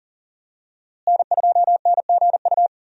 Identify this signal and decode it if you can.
Morse code, heard as a keyed tone, about 38 wpm, 700 Hz. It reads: N2NGU
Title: USA Call Signs - 5 Characters - Repeated 3x - 40wpm